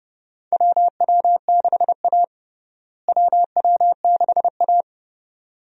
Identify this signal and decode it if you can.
Morse code, heard as a keyed tone, about 30 wpm, 700 Hz. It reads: WW6A WW6A
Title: USA Call Signs - 4 Characters - Repeated 2x - 30wpm